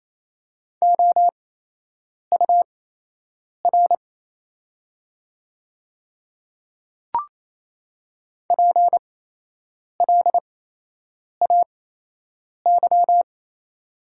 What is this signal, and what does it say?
Morse code, heard as a keyed tone, about 27 wpm, spaced as for 8 wpm, 700 Hz. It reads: OUR PLAY
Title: 8x Character Spacing - Top 200 Words - 28wpm